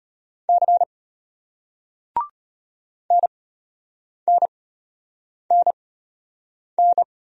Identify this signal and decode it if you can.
Morse code, heard as a keyed tone, about 34 wpm, spaced as for 8 wpm, 700 Hz. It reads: C NNNN